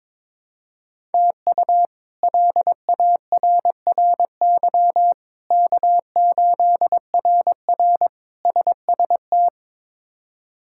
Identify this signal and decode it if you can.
Morse code, heard as a keyed tone, about 22 wpm, 700 Hz. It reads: TU LARRY K8RR SST
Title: K1USN SST QSO Practice - 22wpm